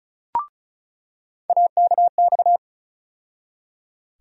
Morse code, heard as a keyed tone, about 34 wpm, 700 Hz. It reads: AKX
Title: USA Call Sign Postfixes - 3 Letters 35wpm